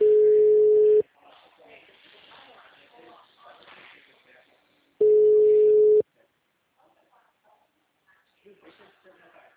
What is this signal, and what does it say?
Ringback tone (Europe, 425 Hz), with speech.
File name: out- -2005- - - .21072